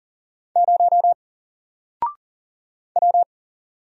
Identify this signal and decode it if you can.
Morse code, heard as a keyed tone, about 39 wpm, 700 Hz. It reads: 0 W
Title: Single Character (Letter:Number:Punctuation:Prosign) 40wpm